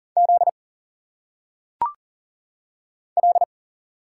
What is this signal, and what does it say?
Morse code, heard as a keyed tone, about 40 wpm, 700 Hz. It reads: Z L
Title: Single Letters - Phonetic Pronunciation 40wpm